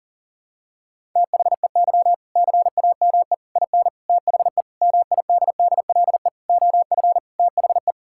Morse code, heard as a keyed tone, about 40 wpm, 700 Hz. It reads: THEY CAME IN THE MIDDLE OF THE
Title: Sentences from Top 1000 Words - Review Part 6 - 40wpm